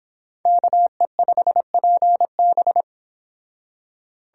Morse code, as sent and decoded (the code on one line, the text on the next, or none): -.- . ..... .--. -...
KE5PB